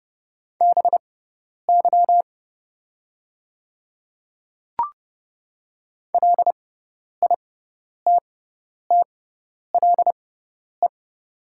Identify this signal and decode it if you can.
Morse code, heard as a keyed tone, about 30 wpm, spaced as for 10 wpm, 700 Hz. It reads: BY LITTLE
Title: 6x Character Spacing - Top 200 Words - 30wpm